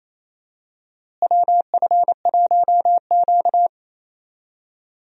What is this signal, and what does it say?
Morse code, heard as a keyed tone, about 28 wpm, 700 Hz. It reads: WF1Q
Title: USA Call Signs - 4 Characters - Repeated 3x - 28wpm